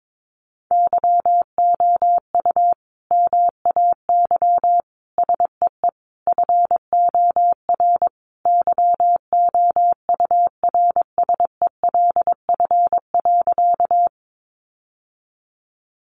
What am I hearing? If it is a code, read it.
Morse code, heard as a keyed tone, about 22 wpm, 700 Hz. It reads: YOU MAY SEE FOR YOURSELF.
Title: Sentences from Top 600 Words 22wpm